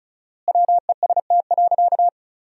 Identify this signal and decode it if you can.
Morse code, heard as a keyed tone, about 34 wpm, 700 Hz. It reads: WEST.
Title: 4x Word-Spacing - Sentences from Top 400 Words - 35wpm